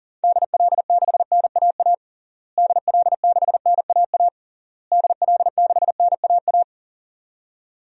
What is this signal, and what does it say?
Morse code, heard as a keyed tone, about 40 wpm, 700 Hz. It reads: DL6NAA DL6NAA DL6NAA
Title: DXCC-100 Call Signs - 6 Characters - Repeated 3x - 40wpm